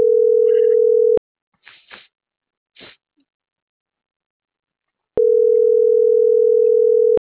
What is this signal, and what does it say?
Ringback tone (North America), 440 + 480 Hz.
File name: CENTRAL MAERSK / out andres.rdzmsk